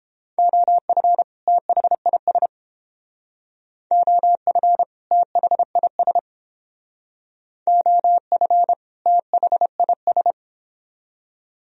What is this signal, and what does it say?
Morse code, heard as a keyed tone, about 31 wpm, 700 Hz. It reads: OF THIS OF THIS OF THIS